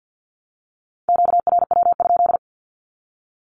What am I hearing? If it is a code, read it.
Morse code, heard as a keyed tone, about 48 wpm, 700 Hz. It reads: QRW?
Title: Q-Codes Repeated 3x 50wpm